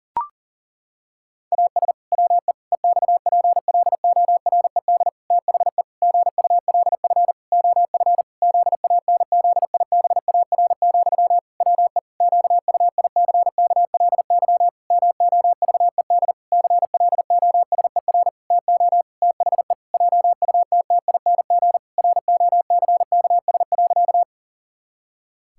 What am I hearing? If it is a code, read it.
Morse code, heard as a keyed tone, about 37 wpm, 700 Hz. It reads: AS WE EXPLORED THE GULF OF ZANZIBAR, WE QUICKLY MOVED CLOSER TO THE JUTTING ROCKS.
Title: Pangram Challenge v2 - 40wpm